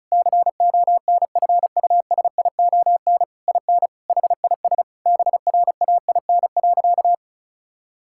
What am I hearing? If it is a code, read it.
Morse code, heard as a keyed tone, about 36 wpm, 700 Hz. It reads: CONFUSION IN HIS BRAIN.
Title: Sentences from Top 1600 Words 35wpm